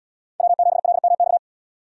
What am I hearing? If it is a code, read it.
Morse code, heard as a keyed tone, about 77 wpm, 700 Hz. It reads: W7LMX